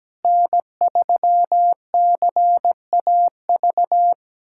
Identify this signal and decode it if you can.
Morse code, heard as a keyed tone, about 17 wpm, 700 Hz. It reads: N3CAV
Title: USA Callsign DE Callsign 17wpm